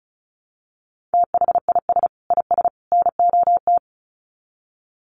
Morse code, heard as a keyed tone, about 34 wpm, 700 Hz. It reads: THIS IS NOT
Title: Sets of 3 Words 35wpm